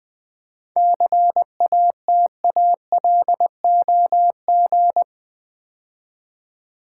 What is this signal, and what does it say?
Morse code, heard as a keyed tone, about 20 wpm, 700 Hz. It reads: CATALOG
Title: 7 Letter Words - First 3 Letters Given - 20wpm